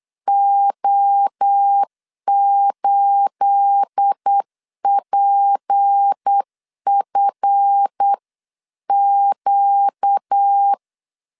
Morse code, heard as a keyed tone, about 8 wpm, 800 Hz. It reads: O8PFQ
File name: CW / DET / 1 9 1500C S B 8wpm 5624